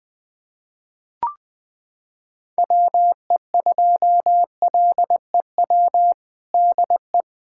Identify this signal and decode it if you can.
Morse code, heard as a keyed tone, about 20 wpm, 700 Hz. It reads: WE2LEW DE
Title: USA Callsign DE Callsign - 6 Characters - 20wpm